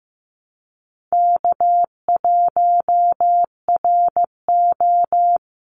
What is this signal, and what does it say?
Morse code, heard as a keyed tone, about 15 wpm, 700 Hz. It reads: K1RO
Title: Easy Call Signs - 15wpm